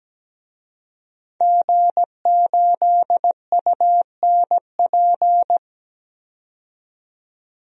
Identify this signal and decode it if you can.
Morse code, heard as a keyed tone, about 17 wpm, 700 Hz. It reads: G8UNP
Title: DXCC-100 Call Signs - 5 Characters - 17wpm